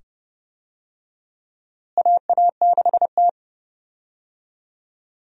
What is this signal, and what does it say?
Morse code, heard as a keyed tone, about 31 wpm, 700 Hz. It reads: AA6T
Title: USA Call Signs - 4 Characters - Repeated 2x - 30wpm